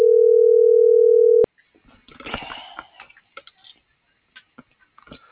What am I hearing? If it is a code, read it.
Ringback tone (North America), 440 + 480 Hz.
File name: CENTRAL MABE / out luis.palacios